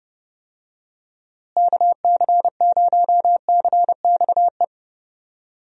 Morse code, heard as a keyed tone, about 30 wpm, 700 Hz. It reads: KC0CXE
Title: USA Call Signs - 6 Characters - Repeated 2x - 30wpm